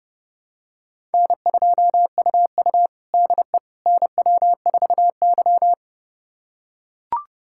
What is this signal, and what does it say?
Morse code, heard as a keyed tone, about 30 wpm, 700 Hz. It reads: N2UU DE NW4Y
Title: USA Callsign DE Callsign - 4 Characters - 30wpm